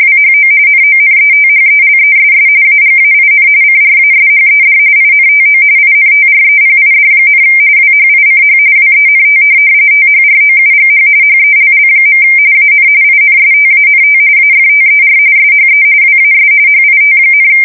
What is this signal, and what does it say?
RTTY, about 45.45 baud. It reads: WHO NONETHELESS AGAIN CRASHED INTO INTREPID, WHICH WENT THROUGH HELL. AN INSTANT AFTER SHE WAS HIT,
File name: fsk170 / USS-New-Jersey